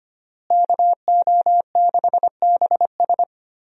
Morse code, heard as a keyed tone, about 25 wpm, 700 Hz. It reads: KO6BS
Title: USA Call Signs - 5 Characters - Repeated 3x - 25wpm